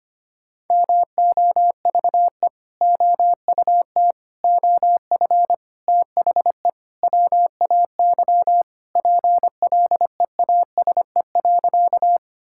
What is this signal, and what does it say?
Morse code, heard as a keyed tone, about 25 wpm, 700 Hz. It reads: MOVE OUT OF THE WAY PLEASE.
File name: practice / 25wpm / Sentences with 4-Word Phrases - 25wpm